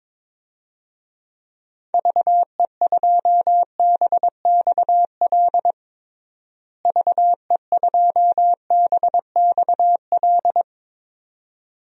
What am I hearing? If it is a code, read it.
Morse code, heard as a keyed tone, about 21 wpm, 700 Hz. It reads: VE2BXL VE2BXL
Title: DXCC-100 Call Signs - 6 Characters - Repeated 2x - 22wpm